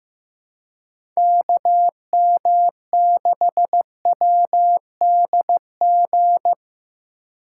Morse code, heard as a keyed tone, about 15 wpm, 700 Hz. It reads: KM6WDG
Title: USA Call Signs - 6 Characters - 4x Character Spacing - 15wpm